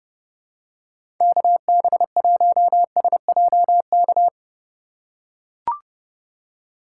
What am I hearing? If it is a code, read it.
Morse code, heard as a keyed tone, about 30 wpm, 700 Hz. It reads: KB1SJK
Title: USA Call Signs - 6 Characters - 2x Character Spacing - 30wpm